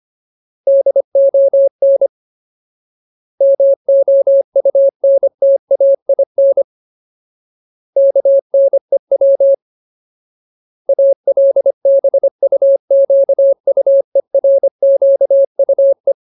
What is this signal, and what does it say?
Morse code, heard as a keyed tone, about 25 wpm, 550 Hz. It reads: DON MOUNTAIN KNEW ALBUQUERQUE
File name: Ultimate Word List / 25WPM Ultimate